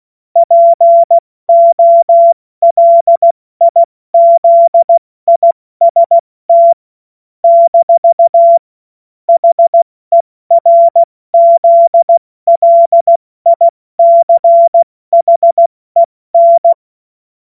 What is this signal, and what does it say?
Morse code, heard as a keyed tone, about 16 wpm, spaced as for 14 wpm, 680 Hz. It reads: POLIZIST - HERZLICHEN
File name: Morsekurs / Witze / Witz 26. Oktober 2021, 70 F80